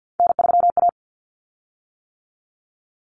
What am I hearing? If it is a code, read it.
Morse code, heard as a keyed tone, about 50 wpm, 700 Hz. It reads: N3A
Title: Common Cut Numbers - 3-Digit 50wpm